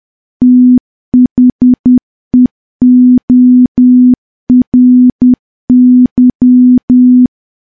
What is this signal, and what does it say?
Morse code, heard as a keyed tone, about 10 wpm, 250 Hz. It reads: THEORY
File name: morse / morse-training-10WPM